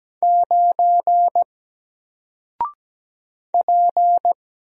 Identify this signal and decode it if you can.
Morse code, heard as a keyed tone, about 17 wpm, 700 Hz. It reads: 9 P
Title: Single Character (Letter:Number:Punctuation:Prosign) Alt 17wpm